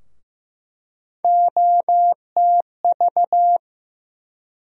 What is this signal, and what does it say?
Morse code, heard as a keyed tone, about 15 wpm, 700 Hz. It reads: OTV
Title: USA Call Sign Postfixes - 3 Letters 15wpm